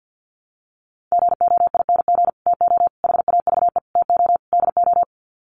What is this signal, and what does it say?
Morse code, heard as a keyed tone, about 49 wpm, 700 Hz. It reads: GOING TO HAVE TO DO
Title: Sets of 5 Words Encore 50wpm